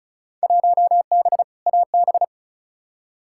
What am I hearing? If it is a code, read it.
Morse code, heard as a keyed tone, about 36 wpm, 700 Hz. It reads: 1B AB
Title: Field Day Exchanges - 35wpm